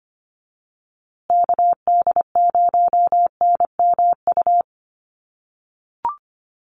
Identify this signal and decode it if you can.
Morse code, heard as a keyed tone, about 25 wpm, 700 Hz. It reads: KD0NMU
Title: USA Call Signs Repeated 2x 25wpm